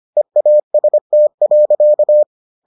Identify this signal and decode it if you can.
Morse code, heard as a keyed tone, about 25 wpm, 600 Hz. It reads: EAST.